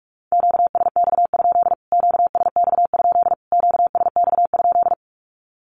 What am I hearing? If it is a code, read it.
Morse code, heard as a keyed tone, about 44 wpm, 700 Hz. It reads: QSX? QSX? QSX?
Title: Q-Codes Repeated 3x 45wpm